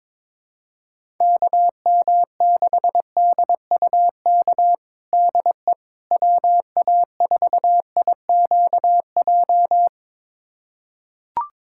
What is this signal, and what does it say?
Morse code, heard as a keyed tone, about 22 wpm, 700 Hz. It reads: KM6DUK DE WA4IQJ
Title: USA Callsign DE Callsign - 6 Characters - 22wpm